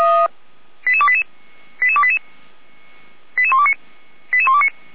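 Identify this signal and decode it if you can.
Five-tone selective call (ZVEI), heard as quick runs of five tones, tones of about 70 ms. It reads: 88288 88288 88129
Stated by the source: ZVEI five-tone selective call received on SDR.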